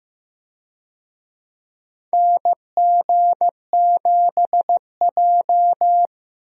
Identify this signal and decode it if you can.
Morse code, heard as a keyed tone, about 15 wpm, 700 Hz. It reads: NG7J
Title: USA Call Signs - 4 Characters - 15wpm